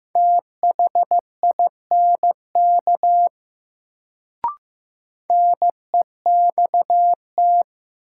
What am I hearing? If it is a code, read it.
Morse code, heard as a keyed tone, about 15 wpm, 700 Hz. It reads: THINK NEXT
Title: Sets of 2 Words - New Word Drill - 15wpm